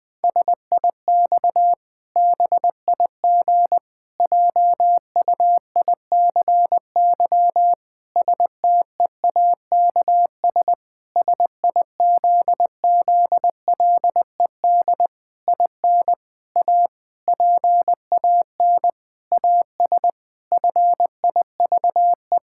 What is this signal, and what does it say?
Morse code, heard as a keyed tone, about 20 wpm, 700 Hz. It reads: SIX BIG JUICY STEAKS SIZZLED IN A PAN AS FIVE
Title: Pangram Challenge v2 - 20wpm